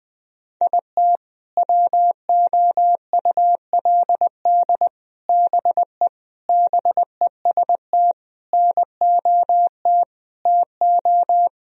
Morse code, heard as a keyed tone, about 21 wpm, 700 Hz. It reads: IT WOULD BE BEST NOT TO
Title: Sentences from Top 1000 Words - Review Part 1 - 20wpm